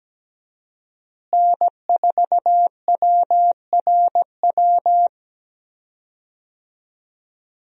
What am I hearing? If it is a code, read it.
Morse code, heard as a keyed tone, about 17 wpm, 700 Hz. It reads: N4WRW
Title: USA Call Signs - 5 Characters - Repeated 3x - 17wpm